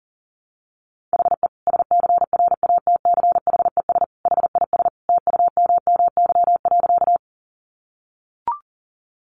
Morse code, heard as a keyed tone, about 40 wpm, 700 Hz. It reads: HE SCRATCHES HIS TUMMY.
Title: Easy Phrases 40wpm